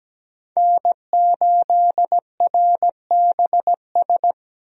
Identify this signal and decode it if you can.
Morse code, heard as a keyed tone, about 17 wpm, 700 Hz. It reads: N8RBS
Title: USA Call Signs - 5 Characters - 17wpm